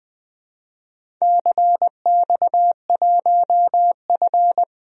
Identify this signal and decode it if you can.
Morse code, heard as a keyed tone, about 20 wpm, 700 Hz. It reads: CX1F
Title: DXCC-100 Call Signs - 4 Characters - 20wpm